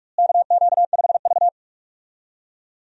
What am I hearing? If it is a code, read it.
Morse code, heard as a keyed tone, about 46 wpm, 700 Hz. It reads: KQ5V
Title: USA Call Signs - 4 Characters - Repeated 2x - 45wpm